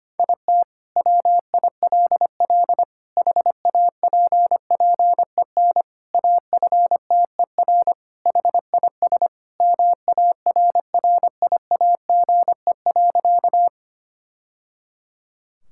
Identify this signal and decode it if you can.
Morse code, heard as a keyed tone, about 25 wpm, 700 Hz. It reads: IT WILL HAPPEN AFTER HIS MARRIAGE.
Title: Sentences from Top 1000 Words - Review Part 5 - 25wpm